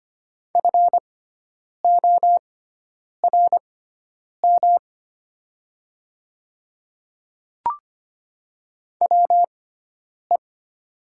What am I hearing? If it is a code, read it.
Morse code, heard as a keyed tone, about 25 wpm, spaced as for 9 wpm, 700 Hz. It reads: FORM WE
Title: 6x Character Spacing - Top 200 Words - 25wpm